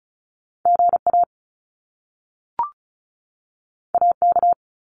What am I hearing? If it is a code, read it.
Morse code, heard as a keyed tone, about 36 wpm, 700 Hz. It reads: GA AK
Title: US States 35wpm